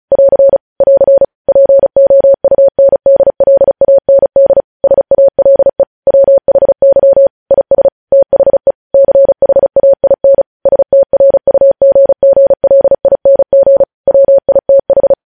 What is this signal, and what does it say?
Morse code, heard as a keyed tone, about 36 wpm, 550 Hz. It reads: <AR> <AR> POUNDLAND SALE WHY IS THE CHAIN STRUGGLING WITH
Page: news headlines in morse code at 35 wpm